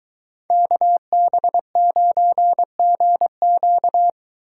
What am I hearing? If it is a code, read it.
Morse code, heard as a keyed tone, about 23 wpm, 700 Hz. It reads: KB9GQ